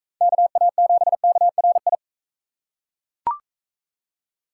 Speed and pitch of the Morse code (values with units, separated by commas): 41 wpm, 700 Hz